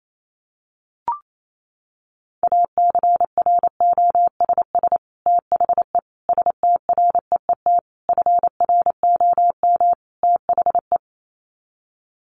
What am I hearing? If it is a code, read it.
Morse code, heard as a keyed tone, about 28 wpm, 700 Hz. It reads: ACROSS THE STREET FROM THE
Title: Sets of 5 Words Encore 28wpm